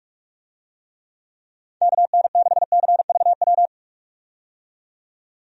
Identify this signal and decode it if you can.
Morse code, heard as a keyed tone, about 46 wpm, 700 Hz. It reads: KN6CVW
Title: USA Call Signs - 6 Characters - 45wpm